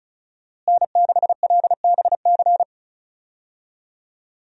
Morse code, heard as a keyed tone, about 36 wpm, 700 Hz. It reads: N6LBC
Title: USA Call Signs - 5 Characters - Repeated 3x - 35wpm